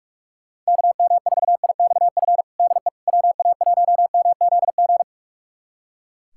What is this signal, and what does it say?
Morse code, heard as a keyed tone, about 41 wpm, 700 Hz. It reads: KM4IXF DE WA1MZG
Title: USA Callsign DE Callsign 45wpm